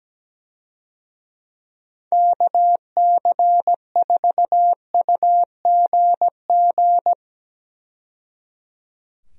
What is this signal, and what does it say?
Morse code, heard as a keyed tone, about 17 wpm, 700 Hz. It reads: KC4UGG
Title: USA Call Signs - 6 Characters - 17wpm